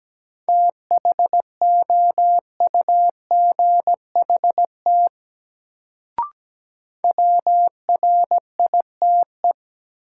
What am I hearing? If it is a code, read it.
Morse code, heard as a keyed tone, about 17 wpm, 700 Hz. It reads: THOUGHT WRITE
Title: Sentences from Top 300 Words - New Word Drill - 17wpm